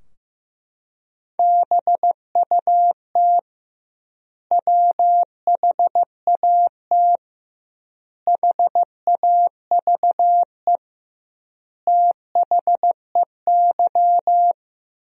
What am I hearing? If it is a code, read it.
Morse code, heard as a keyed tone, about 16 wpm, 700 Hz. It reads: BUT WHAT HAVE THEY
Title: 2x Word-Spacing - Sentences from Top 100 Words - 15wpm